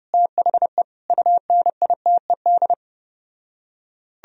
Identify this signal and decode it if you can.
Morse code, heard as a keyed tone, about 33 wpm, 700 Hz. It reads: THE UNITED